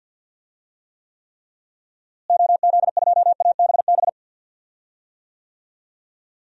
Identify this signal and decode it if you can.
Morse code, heard as a keyed tone, about 50 wpm, 700 Hz. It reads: OZ2ABB
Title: DXCC-100 Call Signs - 6 Characters - Repeated 3x - 50wpm